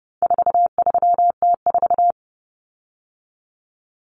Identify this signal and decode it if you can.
Morse code, heard as a keyed tone, about 30 wpm, 700 Hz. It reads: 43T4
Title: Common Cut Numbers - 4-Digit 30wpm